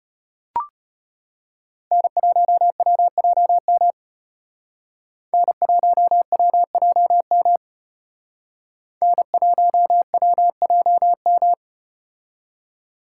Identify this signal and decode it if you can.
Morse code, heard as a keyed tone, about 34 wpm, 700 Hz. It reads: N1WJM N1WJM N1WJM